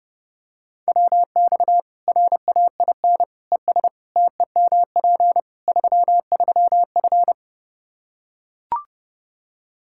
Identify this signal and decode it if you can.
Morse code, heard as a keyed tone, about 30 wpm, 700 Hz. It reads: WX RAIN ES TEMP 33F
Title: Practice QSO - Exchange 2 - Info - 30wpm